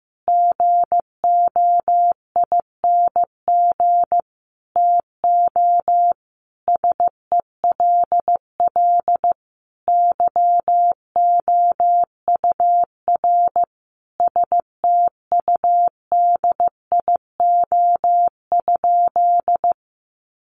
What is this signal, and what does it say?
Morse code, heard as a keyed tone, about 15 wpm, 700 Hz. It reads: GOING TO SELL YOUR STUDIO?
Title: Sentences from Top 1200 Words 15wpm